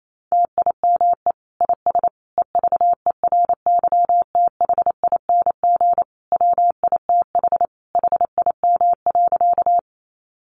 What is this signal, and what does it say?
Morse code, heard as a keyed tone, about 28 wpm, 700 Hz. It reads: TIME IS EVERYTHING WITH HIM.
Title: Sentences from Top 700 Words 28wpm